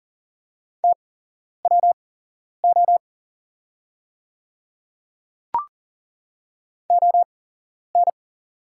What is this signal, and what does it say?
Morse code, heard as a keyed tone, about 40 wpm, spaced as for 11 wpm, 690 Hz. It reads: TWO ON
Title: 8x Character Spacing - Top 300 Words - 40wpm